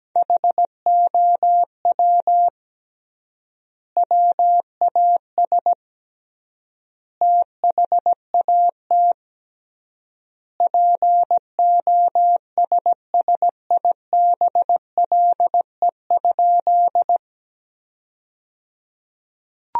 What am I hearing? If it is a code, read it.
Morse code, heard as a keyed tone, about 17 wpm, 700 Hz. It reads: HOW WAS THAT POSSIBLE?
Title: 3x Word-Spacing - Sentences from Top 500 Words - 17wpm